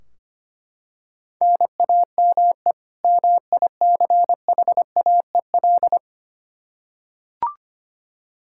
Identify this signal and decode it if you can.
Morse code, heard as a keyed tone, about 25 wpm, 700 Hz. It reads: NAME MICHAEL
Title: Practice QSO - Exchange 1 - Beginning - 25wpm